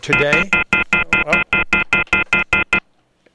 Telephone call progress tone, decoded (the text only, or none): off-hook warning tone